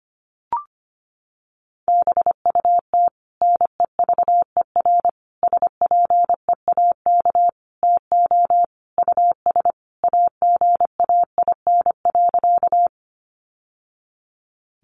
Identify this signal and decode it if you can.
Morse code, heard as a keyed tone, about 25 wpm, 700 Hz. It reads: BUT NEVER SPEAK TO US AGAIN.
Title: Sentences from Top 1100 Words 25wpm